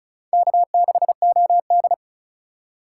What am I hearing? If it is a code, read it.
Morse code, heard as a keyed tone, about 36 wpm, 700 Hz. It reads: K6OD
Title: USA Call Signs - 4 Characters - 2x Character Spacing - 35wpm